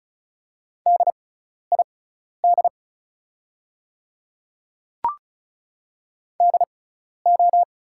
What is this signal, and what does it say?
Morse code, heard as a keyed tone, about 36 wpm, spaced as for 12 wpm, 690 Hz. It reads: DID DO
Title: 6x Character Spacing - Top 100 Words - 35wpm